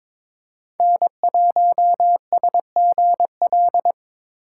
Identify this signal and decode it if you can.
Morse code, heard as a keyed tone, about 22 wpm, 700 Hz. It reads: N1SGL
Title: USA Call Signs - 5 Characters - 3x Character Spacing - 22wpm